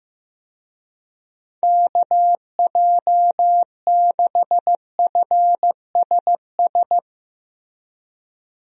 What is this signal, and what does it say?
Morse code, heard as a keyed tone, about 15 wpm, 700 Hz. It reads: KJ6FSS
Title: USA Call Signs - 6 Characters - Repeated 2x - 15wpm